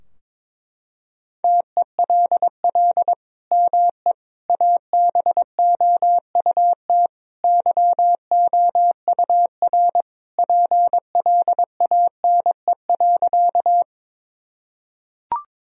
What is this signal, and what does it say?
Morse code, heard as a keyed tone, about 22 wpm, 700 Hz. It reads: TELL ME ABOUT YOUR PLANE.
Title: Sentences from Top 1000 Words - Review Part 3 - 22wpm